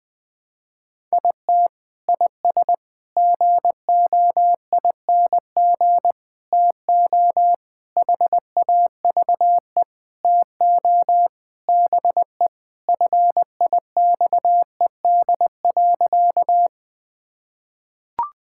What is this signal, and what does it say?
Morse code, heard as a keyed tone, about 19 wpm, 700 Hz. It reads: IT IS GOING TO HAVE TO BE FIXED.
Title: Sentences with 5-Word Phrases - 20wpm